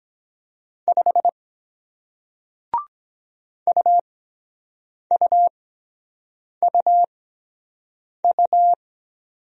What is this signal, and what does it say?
Morse code, heard as a keyed tone, about 23 wpm, spaced as for 7 wpm, 700 Hz. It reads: 5 UUUU